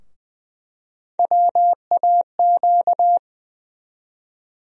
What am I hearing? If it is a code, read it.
Morse code, heard as a keyed tone, about 20 wpm, 700 Hz. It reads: WAQ
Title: USA Call Sign Postfixes - 3 Letters 20wpm